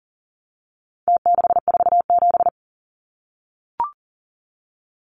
Morse code, heard as a keyed tone, about 40 wpm, 700 Hz. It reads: T647
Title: Numbers - Contest Style - 4-Digit 40wpm